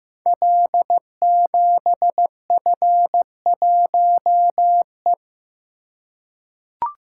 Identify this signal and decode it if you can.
Morse code, heard as a keyed tone, about 15 wpm, 700 Hz. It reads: L7F1E
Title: Code Groups - 5 Letter:Numbers - 15wpm